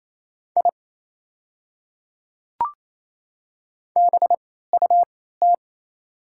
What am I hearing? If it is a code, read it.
Morse code, heard as a keyed tone, about 28 wpm, spaced as for 16 wpm, 700 Hz. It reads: I BUT
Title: 3x Character Spacing - Top 100 Words - 28wpm